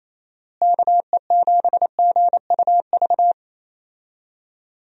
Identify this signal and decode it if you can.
Morse code, heard as a keyed tone, about 28 wpm, 700 Hz. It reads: KE7GUV
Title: USA Call Signs - 6 Characters - Repeated 2x - 28wpm